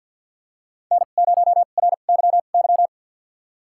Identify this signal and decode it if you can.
Morse code, heard as a keyed tone, about 53 wpm, spaced as for 37 wpm, 700 Hz. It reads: N0RYY
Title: USA Call Signs - 5 Characters - 2x Character Spacing - 50wpm